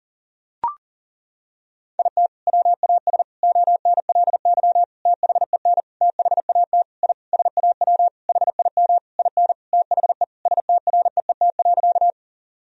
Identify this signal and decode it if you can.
Morse code, heard as a keyed tone, about 40 wpm, 700 Hz. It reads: IT WAS ONLY THEN THAT I SAW HIM IN THE STREET.
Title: Sentences from Top 1000 Words - Review Part 3 - 40wpm